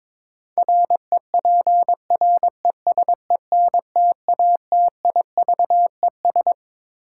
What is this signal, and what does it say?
Morse code, heard as a keyed tone, about 22 wpm, 700 Hz. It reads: REPRESENTATIVES
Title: Sets of 5 Words Encore - New Word Drill - 22wpm